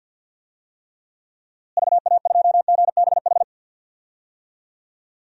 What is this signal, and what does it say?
Morse code, heard as a keyed tone, about 51 wpm, 700 Hz. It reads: VA2GBH
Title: DXCC-100 Call Signs - 6 Characters - Repeated 3x - 50wpm